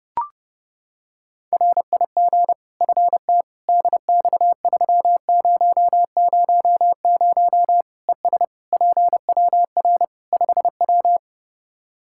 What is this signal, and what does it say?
Morse code, heard as a keyed tone, about 32 wpm, 700 Hz. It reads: RIG FT DX3000 ES PWR 5W
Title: K1USN SST QSO Practice 30wpm